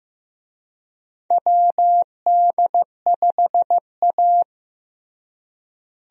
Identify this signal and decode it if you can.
Morse code, heard as a keyed tone, about 15 wpm, 700 Hz. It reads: WD5A
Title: USA Call Signs - 4 Characters - Repeated 3x - 15wpm